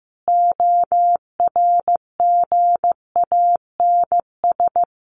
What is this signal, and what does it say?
Morse code, heard as a keyed tone, about 15 wpm, 700 Hz. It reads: ORGANS
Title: A Sound Of Thunder 15wpm